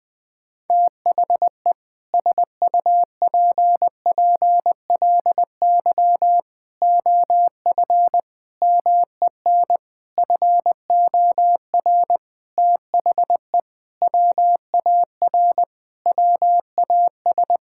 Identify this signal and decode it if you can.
Morse code, heard as a keyed tone, about 20 wpm, 700 Hz. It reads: THE SUPPLY OF MEN FOR THE WAR WAS
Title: Sentences from Top 1000 Words - Review Part 6 - 20wpm